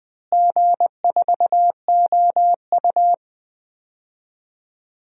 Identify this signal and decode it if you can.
Morse code, heard as a keyed tone, about 20 wpm, 700 Hz. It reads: G4OU
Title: DXCC-100 Call Signs - 4 Characters - Repeated 2x - 20wpm